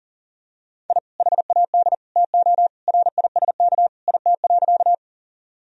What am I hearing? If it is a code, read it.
Morse code, heard as a keyed tone, about 37 wpm, 700 Hz. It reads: I HAD TO RISK IT.
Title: Sentences from Top 1000 Words - Review Part 4 - 40wpm